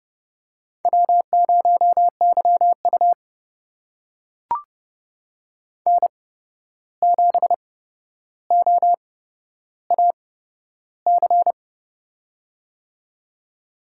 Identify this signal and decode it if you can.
Morse code, heard as a keyed tone, about 29 wpm, spaced as for 12 wpm, 700 Hz. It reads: W0YU N7OAC
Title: USA Call Signs - 8x Character Spacing - 30wpm